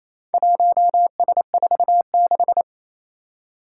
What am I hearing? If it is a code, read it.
Morse code, heard as a keyed tone, about 28 wpm, 700 Hz. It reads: 1S46
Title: Four Challenging Letter-Number Combinations 28wpm